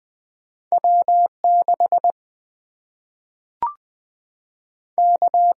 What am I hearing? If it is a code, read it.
Morse code, heard as a keyed tone, about 20 wpm, 700 Hz. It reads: W 6 K
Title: Single-Letter Single-Number 20wpm